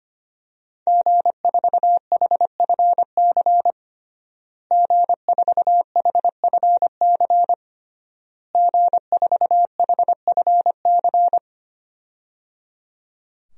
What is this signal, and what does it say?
Morse code, heard as a keyed tone, about 25 wpm, 700 Hz. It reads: G4HFC G4HFC G4HFC
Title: DXCC-100 Call Signs - 5 Characters - Repeated 3x - 25wpm